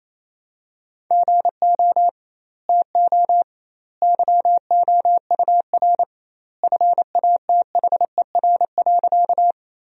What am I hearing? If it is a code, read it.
Morse code, heard as a keyed tone, about 28 wpm, 700 Hz. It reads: GO TO YOUR FATHER.
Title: 2x Word-Spacing - Sentences from Top 300 Words - 28wpm